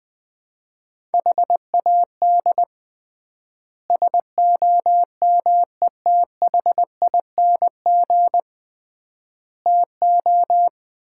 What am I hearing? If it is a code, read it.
Morse code, heard as a keyed tone, about 20 wpm, 700 Hz. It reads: HAD SOMETHING TO
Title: 3x Word-Spacing - Sets of 5 Words - 20wpm